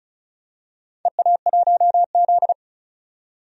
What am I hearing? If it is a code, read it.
Morse code, heard as a keyed tone, about 34 wpm, 700 Hz. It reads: EA1Z
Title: DXCC-100 Call Signs - 4 Characters - 35wpm